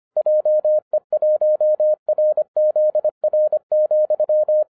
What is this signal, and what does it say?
Morse code, heard as a keyed tone, about 25 wpm, 610 Hz. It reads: JE1RZR,